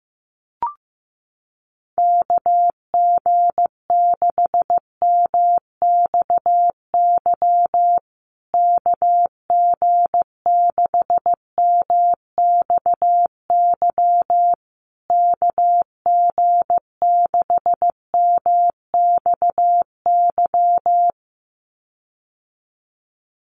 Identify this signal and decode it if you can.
Morse code, heard as a keyed tone, about 15 wpm, 700 Hz. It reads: KG6MXY KG6MXY KG6MXY
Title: USA Call Signs Repeated 3x 15wpm